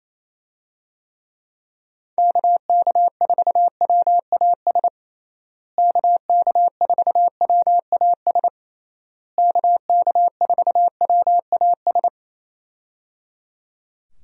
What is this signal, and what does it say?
Morse code, heard as a keyed tone, about 28 wpm, 700 Hz. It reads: KK4WAS KK4WAS KK4WAS
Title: USA Call Signs - 6 Characters - Repeated 3x - 28wpm